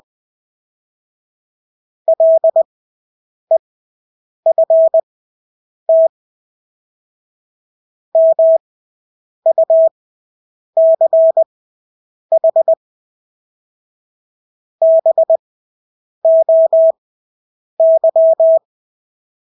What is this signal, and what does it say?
Morse code, heard as a keyed tone, about 20 wpm, spaced as for 8 wpm, 650 Hz. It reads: LEFT MUCH BOY
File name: Engelske ord 20/8 wpm